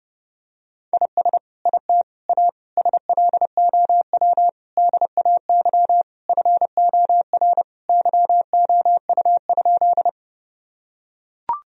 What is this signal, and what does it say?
Morse code, heard as a keyed tone, about 30 wpm, 700 Hz. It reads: IS IT A SLOW DAY FOR YOU?
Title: Sentences from Top 1000 Words - Review Part 2 - 30wpm